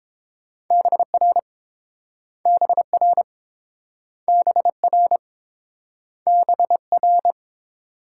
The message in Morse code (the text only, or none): BR BR BR BR